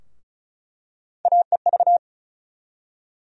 Morse code, heard as a keyed tone, about 34 wpm, 700 Hz. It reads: AEV
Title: USA Call Sign Postfixes - 3 Letters 35wpm